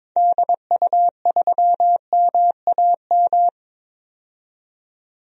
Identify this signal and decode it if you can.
Morse code, heard as a keyed tone, about 22 wpm, 700 Hz. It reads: DU3MAM
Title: DXCC-100 Call Signs - 6 Characters - 22wpm